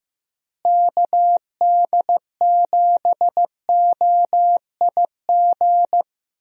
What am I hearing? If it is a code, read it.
Morse code, heard as a keyed tone, about 15 wpm, 700 Hz. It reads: KD7OIG
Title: USA Call Signs - 6 Characters - Repeated 3x - 15wpm